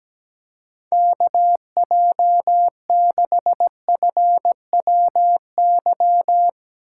Morse code, heard as a keyed tone, about 19 wpm, 700 Hz. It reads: KJ6FWY
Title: Speed Racing - USA Call Signs 17wpm